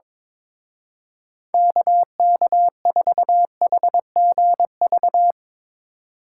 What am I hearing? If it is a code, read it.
Morse code, heard as a keyed tone, about 23 wpm, 700 Hz. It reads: KK4HGV
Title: Speed Racing - USA 2x3 Call Signs 22wpm